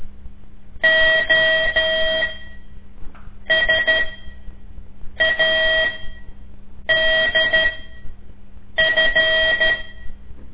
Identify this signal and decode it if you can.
Morse code, heard as a keyed tone, about 13 wpm, spaced as for 6 wpm, 640 Hz. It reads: OSADF
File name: ik1weg2 / CORSI CW / lez5-2